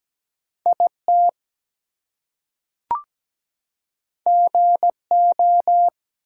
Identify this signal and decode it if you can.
Morse code, heard as a keyed tone, about 17 wpm, 700 Hz. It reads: IT GO
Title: Two Letter Words - 17wpm